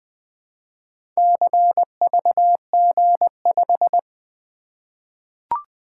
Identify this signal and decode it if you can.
Morse code, heard as a keyed tone, about 20 wpm, 700 Hz. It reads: CVG5
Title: Four Letter-Number Combinations - 20wpm